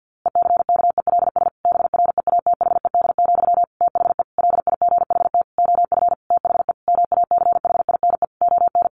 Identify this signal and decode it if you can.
Morse code, heard as a keyed tone, about 50 wpm, 700 Hz. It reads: ECKELS BREATHED, THE LIGHT OF THE MACHINE ON
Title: A Sound Of Thunder 50wpm